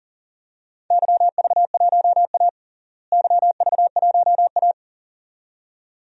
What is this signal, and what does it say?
Morse code, heard as a keyed tone, about 40 wpm, 700 Hz. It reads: YV1A YV1A
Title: DXCC-100 Call Signs - 4 Characters - Repeated 2x - 40wpm